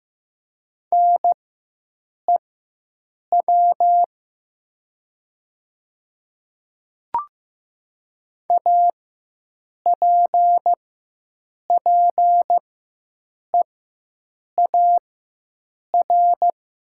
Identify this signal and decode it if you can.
Morse code, heard as a keyed tone, about 15 wpm, spaced as for 7 wpm, 700 Hz. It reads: NEW APPEAR
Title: 4x Character Spacing - Top 500 Words - 15wpm